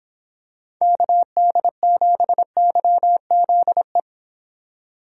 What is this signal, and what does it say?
Morse code, heard as a keyed tone, about 26 wpm, 700 Hz. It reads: KD7YZE